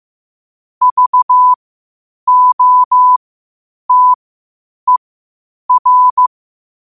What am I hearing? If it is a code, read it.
Morse code, heard as a keyed tone, about 15 wpm, spaced as for 8 wpm, 1000 Hz. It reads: VOTER
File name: qotdfiles / 8wpm